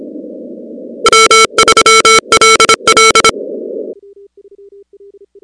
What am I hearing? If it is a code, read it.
Morse code, heard as a keyed tone, about 26 wpm, 1340 Hz, with noise. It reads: W3LL
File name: contests / 2019-cw-wpx-kt5j-k5pi- -2330 0-16